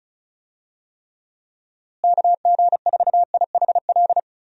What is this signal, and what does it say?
Morse code, heard as a keyed tone, about 36 wpm, 700 Hz. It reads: KG4IHL
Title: USA Call Signs - 6 Characters - Repeated 2x - 35wpm